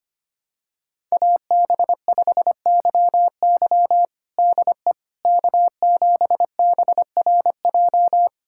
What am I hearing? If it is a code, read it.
Morse code, heard as a keyed tone, about 25 wpm, 700 Hz. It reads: AB5YY DE K7BRJ
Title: USA Callsign DE Callsign - 5 Characters - 25wpm